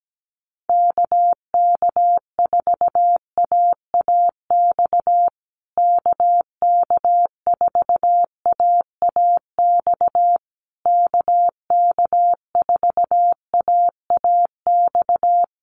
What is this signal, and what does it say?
Morse code, heard as a keyed tone, about 17 wpm, 700 Hz. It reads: KK4AAX KK4AAX KK4AAX
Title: USA Call Signs Repeated 3x 17wpm